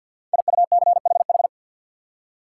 Morse code, heard as a keyed tone, about 51 wpm, 700 Hz. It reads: IUCHH
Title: Code Groups - 5 Letters - 50wpm